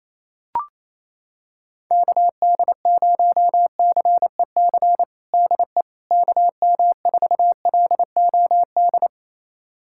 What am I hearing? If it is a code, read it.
Morse code, heard as a keyed tone, about 28 wpm, 700 Hz. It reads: KD0CEC DE KM4LOD
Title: USA Callsign DE Callsign - 6 Characters - 28wpm